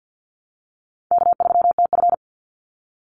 Morse code, heard as a keyed tone, about 50 wpm, 700 Hz. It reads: K3TF
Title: Call Signs Repeated 2x - 50wpm